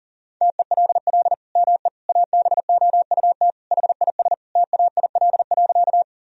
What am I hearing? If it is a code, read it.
Morse code, heard as a keyed tone, about 40 wpm, 700 Hz. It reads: TELL ME ABOUT HIS TAIL.
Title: Sentences from Top 1000 Words - Review Part 3 - 40wpm